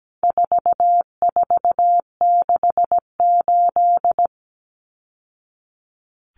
Morse code, heard as a keyed tone, about 17 wpm, 700 Hz. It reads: 4468
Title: Numbers - 4-Digit 17wpm